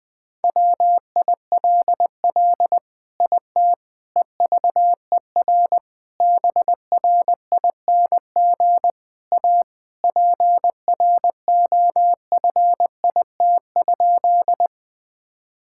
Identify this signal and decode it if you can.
Morse code, heard as a keyed tone, about 20 wpm, 700 Hz. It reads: WILL IT EVER BRING A PROFIT?
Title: Sentences from Top 1000 Words - Review Part 5 - 20wpm